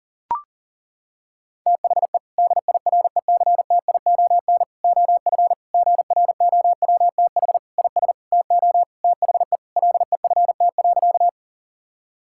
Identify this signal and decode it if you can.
Morse code, heard as a keyed tone, about 40 wpm, 700 Hz. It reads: THE DIRECTION OF GROWTH IS TO THE LEFT.
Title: Sentences from Top 1000 Words - Review Part 5 - 40wpm